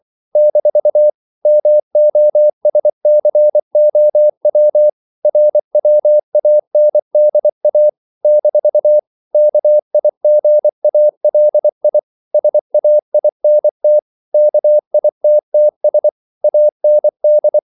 Morse code, heard as a keyed tone, about 24 wpm, 600 Hz. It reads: - MOSCOW RWANDA - KIGALI SAINT KITTS AND
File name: CW / Practice / NationalCapitals24